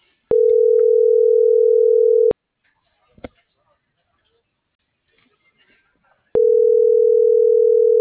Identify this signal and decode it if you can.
Ringback tone (North America), 440 + 480 Hz.
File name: CENTRAL SKYBLUE / out karina.medina - .30844